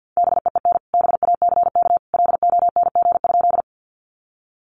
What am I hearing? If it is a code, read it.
Morse code, heard as a keyed tone, about 50 wpm, 700 Hz. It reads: BEEN BACK LONG?
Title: Sentences from Top 200 Words 50wpm